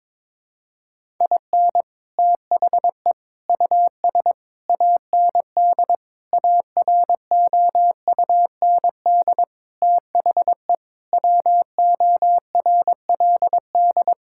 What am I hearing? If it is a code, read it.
Morse code, heard as a keyed tone, about 22 wpm, 700 Hz. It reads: IN THE US AND AROUND THE WORLD
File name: practice / 22wpm / Sets of 7 Words - 22wpm